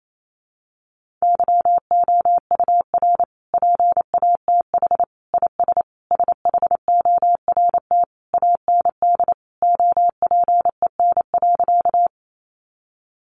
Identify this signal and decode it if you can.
Morse code, heard as a keyed tone, about 28 wpm, 700 Hz. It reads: YOUR PATH IS SHORT AND OPEN.
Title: Sentences from Top 1400 Words 28wpm